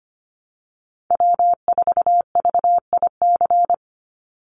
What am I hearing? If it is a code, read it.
Morse code, heard as a keyed tone, about 25 wpm, 700 Hz. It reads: W4VIC
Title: Call Signs Repeated 3x - 25wpm